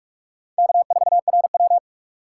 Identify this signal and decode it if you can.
Morse code, heard as a keyed tone, about 44 wpm, 700 Hz. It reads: K4RW
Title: USA Call Signs - 4 Characters - 2x Character Spacing - 45wpm